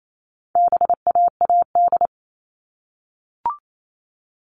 Morse code, heard as a keyed tone, about 28 wpm, 700 Hz. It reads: BAAD
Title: All Cut Numbers - 4-Digit 28wpm